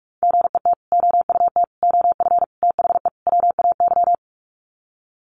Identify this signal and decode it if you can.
Morse code, heard as a keyed tone, about 44 wpm, 700 Hz. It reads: GET OUT OF THE WAY
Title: Sets of 5 Words Encore 45wpm